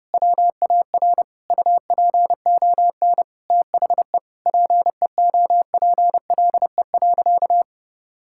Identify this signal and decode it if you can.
Morse code, heard as a keyed tone, about 32 wpm, 700 Hz. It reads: WAR UPON THE PEOPLE.
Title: Sentences from Top 1000 Words - Review Part 6 - 30wpm